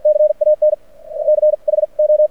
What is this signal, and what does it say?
Morse code, heard as a keyed tone, about 52 wpm, spaced as for 41 wpm, 610 Hz. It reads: CAN CHO
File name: archived / CRSnet